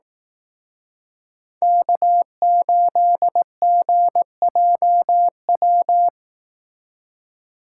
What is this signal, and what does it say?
Morse code, heard as a keyed tone, about 18 wpm, 700 Hz. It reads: K8GJW
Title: Speed Racing - USA 1x3 Call Signs 15wpm